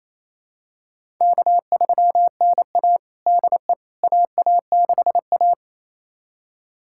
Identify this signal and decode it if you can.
Morse code, heard as a keyed tone, about 28 wpm, 700 Hz. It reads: K3NA DE AA6A
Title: USA Callsign DE Callsign - 4 Characters - 28wpm